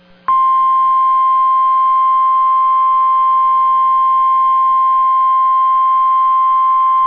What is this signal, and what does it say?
Emergency Alert System (SAME) tones (the 1050 Hz attention tone). It is heard from 1.2 to 5.9 s.